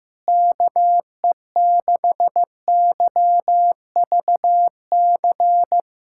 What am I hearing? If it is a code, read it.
Morse code, heard as a keyed tone, about 15 wpm, 700 Hz. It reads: KE6YVC
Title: USA Callsign DE Callsign - 6 Characters - 15wpm